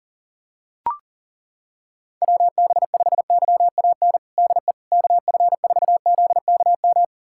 Morse code, heard as a keyed tone, about 40 wpm, 700 Hz. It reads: WB5YAN DE KF4ZKM
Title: USA Callsign DE Callsign 40wpm